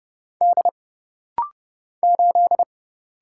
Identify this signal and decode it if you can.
Morse code, heard as a keyed tone, about 31 wpm, 700 Hz. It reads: D 8
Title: Single Letter-Number - Rapid-Fire 30wpm